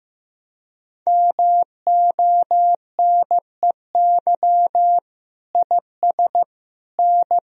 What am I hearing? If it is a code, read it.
Morse code, heard as a keyed tone, about 15 wpm, 700 Hz. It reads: MONEY IS N
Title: Sentences from Top 500 Words and Abbreviations 15wpm